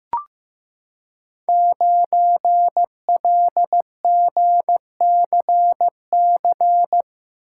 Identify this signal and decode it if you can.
Morse code, heard as a keyed tone, about 15 wpm, 700 Hz. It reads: 9LGCC
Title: Code Groups - 5 Letter:Numbers - 15wpm